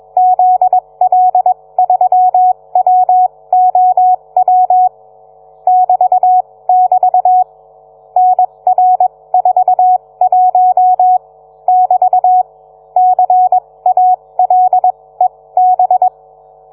Morse code, heard as a keyed tone, about 21 wpm, spaced as for 18 wpm, 740 Hz. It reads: ZL3WOW <BT><BT> NR41 <BT> CALEB